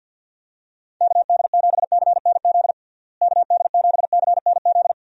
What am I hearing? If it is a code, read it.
Morse code, heard as a keyed tone, about 52 wpm, 700 Hz. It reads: KD7CNZ KD7CNZ
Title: USA Call Signs - 6 Characters - Repeated 2x - 50wpm